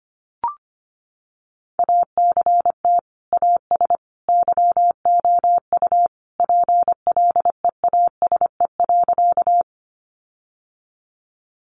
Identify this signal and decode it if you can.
Morse code, heard as a keyed tone, about 25 wpm, 700 Hz. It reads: ACT AS YOU PLEASE.
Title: Sentences from Top 1500 Words 25wpm